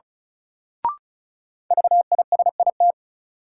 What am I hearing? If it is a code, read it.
Morse code, heard as a keyed tone, about 36 wpm, 700 Hz. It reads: VISIT
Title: Sentences from Top 900 Words - New Word Drill - 35wpm